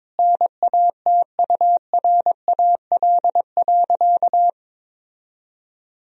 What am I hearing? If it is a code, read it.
Morse code, heard as a keyed tone, about 22 wpm, 700 Hz. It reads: NATURAL.
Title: Sentences from Top 1000 Words - Review Part 4 - 22wpm